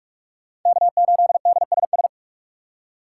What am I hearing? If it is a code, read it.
Morse code, heard as a keyed tone, about 46 wpm, 700 Hz. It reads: K8DSS
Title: K1USN SST QSO Practice - 45wpm